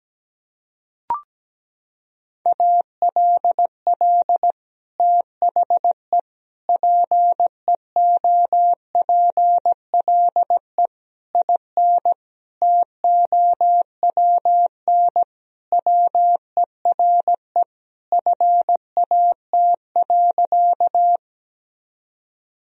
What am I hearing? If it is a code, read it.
Morse code, heard as a keyed tone, about 17 wpm, 700 Hz. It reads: ALL THE PEOPLE IN TOWN WERE FAT.
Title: Sentences from Top 1000 Words - Review Part 4 - 17wpm